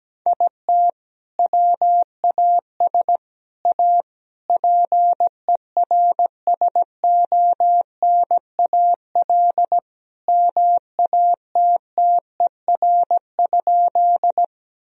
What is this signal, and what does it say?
Morse code, heard as a keyed tone, about 17 wpm, 700 Hz. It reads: IT WAS A PERSONAL MATTER?
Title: Sentences from Top 1000 Words - Review Part 4 - 17wpm